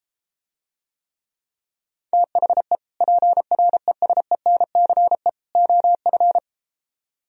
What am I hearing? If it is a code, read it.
Morse code, heard as a keyed tone, about 32 wpm, 700 Hz. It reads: THE PRESENCE OF